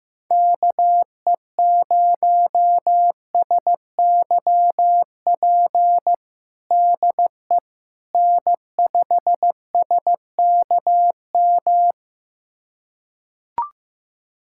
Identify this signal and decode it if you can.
Morse code, heard as a keyed tone, about 15 wpm, 700 Hz. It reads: KE0SYP DE N5SKM
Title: USA Callsign DE Callsign 15wpm